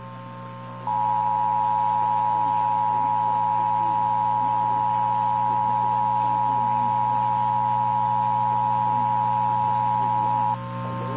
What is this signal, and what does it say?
Emergency Alert System (SAME) tones (the 853 and 960 Hz attention tone), with noise. It is heard from 0.8 to 10.6 s.
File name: eas / KNCYKBIE / KNCYKBIE